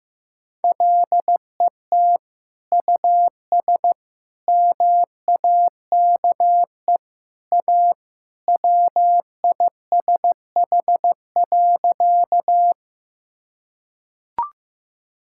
Morse code, heard as a keyed tone, about 15 wpm, 700 Hz. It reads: LET US MAKE A WISH.
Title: Sentences from Top 1600 Words 15wpm